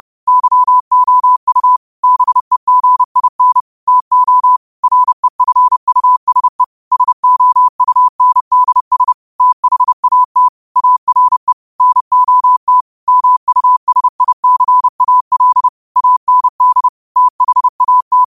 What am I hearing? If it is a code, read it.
Morse code, heard as a keyed tone, about 30 wpm, 1000 Hz. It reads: YOU BEGIN TO REFUSE SOUNDS THAT ARE NOT MUSICAL AND THAT
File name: qotdfiles / 30wpm